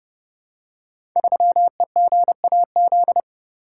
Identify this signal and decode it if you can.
Morse code, heard as a keyed tone, about 32 wpm, 700 Hz. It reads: 3EGAZ
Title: Code Groups - 5 Letter:Numbers - 30wpm